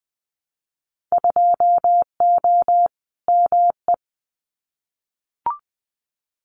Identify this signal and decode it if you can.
Morse code, heard as a keyed tone, about 20 wpm, 700 Hz. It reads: 2O ME
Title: Winter Field Day Exchanges 20wpm